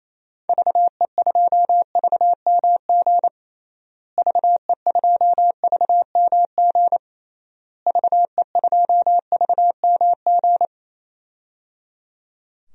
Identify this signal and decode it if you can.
Morse code, heard as a keyed tone, about 28 wpm, 700 Hz. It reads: VE2VMG VE2VMG VE2VMG
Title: DXCC-100 Call Signs - 6 Characters - Repeated 3x - 28wpm